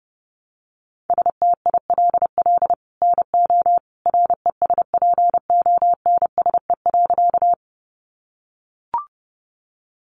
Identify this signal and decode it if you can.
Morse code, heard as a keyed tone, about 30 wpm, 700 Hz. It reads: STILL NO RESPONSE.
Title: Sentences from Top 1400 Words 30wpm